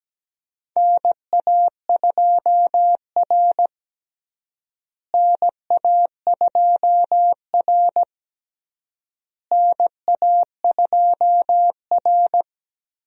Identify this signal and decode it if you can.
Morse code, heard as a keyed tone, about 17 wpm, 700 Hz. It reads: NA2R NA2R NA2R
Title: USA Call Signs - 4 Characters - Repeated 3x - 17wpm